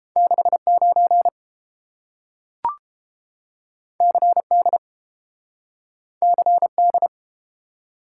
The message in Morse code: 69 CD CD

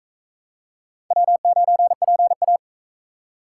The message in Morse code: W9PA

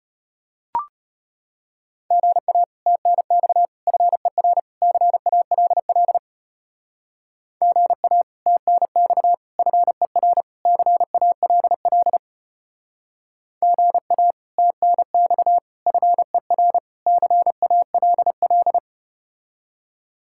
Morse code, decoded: GA TNX FER CALL GA TNX FER CALL GA TNX FER CALL

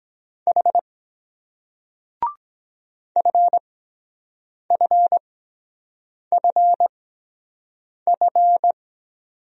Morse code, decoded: H F F F F

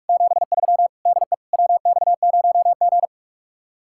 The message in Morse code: 73 DE WX0G

